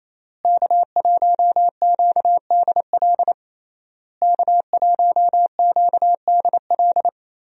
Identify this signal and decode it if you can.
Morse code, heard as a keyed tone, about 28 wpm, 700 Hz. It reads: K1QDL K1QDL
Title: USA Call Signs - 5 Characters - Repeated 3x - 28wpm